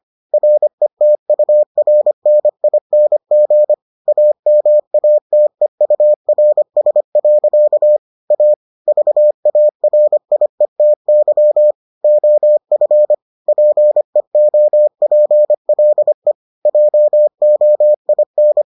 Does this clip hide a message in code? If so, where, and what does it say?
Yes: Morse code from 0.3 to 18.6 s, heard as a keyed tone, about 25 wpm, 600 Hz. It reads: RETURNING AMATEURS. A VARIETY OF PEOPLE JOIN